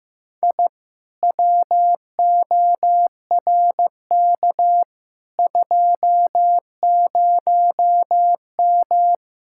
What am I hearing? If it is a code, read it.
Morse code, heard as a keyed tone, about 15 wpm, 700 Hz. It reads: I WORK 20M
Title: Speed-Racing Common QSO Phrases Encore - 15wpm